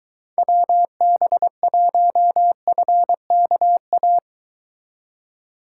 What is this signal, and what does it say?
Morse code, heard as a keyed tone, about 23 wpm, 700 Hz. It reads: WB1FKA